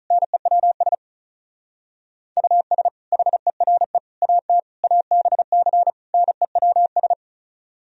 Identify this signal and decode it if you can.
Morse code, heard as a keyed tone, about 45 wpm, 700 Hz. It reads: NEWS US HERE AT ABC NEWS